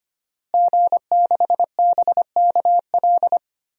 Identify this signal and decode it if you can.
Morse code, heard as a keyed tone, about 25 wpm, 700 Hz. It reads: G6BKL
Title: DXCC-100 Call Signs - 5 Characters - Repeated 3x - 25wpm